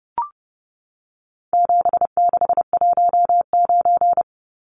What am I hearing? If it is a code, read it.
Morse code, heard as a keyed tone, about 30 wpm, 700 Hz. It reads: 7619
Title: Numbers - 4-Digit 30wpm